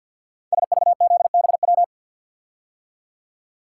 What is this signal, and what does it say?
Morse code, heard as a keyed tone, about 50 wpm, 690 Hz. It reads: SVZBW
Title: Code Groups - 5 Letters - 50wpm